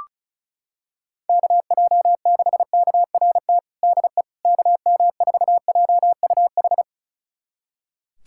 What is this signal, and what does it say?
Morse code, heard as a keyed tone, about 34 wpm, 700 Hz. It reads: KJ6KRT DE KM4JUH
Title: USA Callsign DE Callsign 35wpm